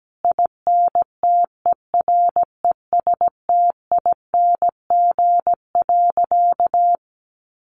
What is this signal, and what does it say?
Morse code, heard as a keyed tone, about 17 wpm, 700 Hz. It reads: INTERESTING.
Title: Sentences from Top 1200 Words 17wpm